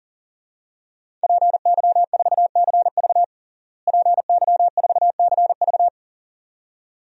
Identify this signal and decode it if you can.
Morse code, heard as a keyed tone, about 37 wpm, 700 Hz. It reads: PY4CV PY4CV
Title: DXCC-100 Call Signs - 5 Characters - Repeated 2x - 40wpm